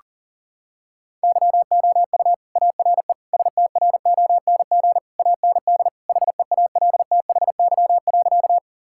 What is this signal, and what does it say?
Morse code, heard as a keyed tone, about 40 wpm, 700 Hz. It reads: YOU ARE STRONG AND HEALTHY.
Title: Sentences from Top 1000 Words - Review Part 5 - 40wpm